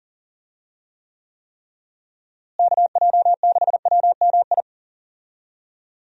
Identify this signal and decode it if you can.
Morse code, heard as a keyed tone, about 40 wpm, 700 Hz. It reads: KJ6WMI
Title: USA Call Signs - 6 Characters - Repeated 3x - 40wpm